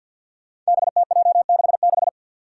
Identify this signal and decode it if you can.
Morse code, heard as a keyed tone, about 49 wpm, 700 Hz. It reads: BTJ66
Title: Five Challenging Letter-Number Combinations 50wpm